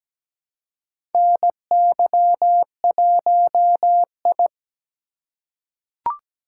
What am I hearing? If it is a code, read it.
Morse code, heard as a keyed tone, about 17 wpm, 700 Hz. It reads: NY1I
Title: USA Call Signs - 4 Characters - 4x Character Spacing - 17wpm